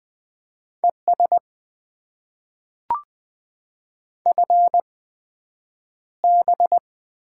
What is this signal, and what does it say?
Morse code, heard as a keyed tone, about 20 wpm, spaced as for 9 wpm, 700 Hz. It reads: ES FB
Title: Top 20 QSO Elements - 8x Character Spacing - 20wpm